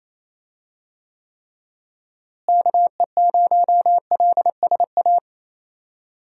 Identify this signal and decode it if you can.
Morse code, heard as a keyed tone, about 28 wpm, 700 Hz. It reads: KE0LSA
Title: USA Call Signs - 6 Characters - Repeated 2x - 28wpm